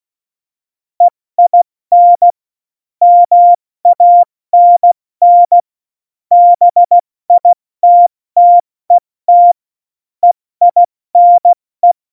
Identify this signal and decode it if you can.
Morse code, heard as a keyed tone, about 16 wpm, 700 Hz. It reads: EIN MANN BITTET EINE
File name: Morsekurs / Witze / Witz 30. Juli 2019, 70 F80